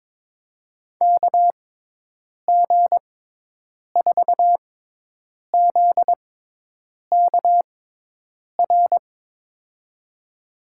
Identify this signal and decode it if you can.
Morse code, heard as a keyed tone, about 21 wpm, spaced as for 8 wpm, 700 Hz. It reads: KG4ZKR
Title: USA Call Signs - 6x Character Spacing - 22wpm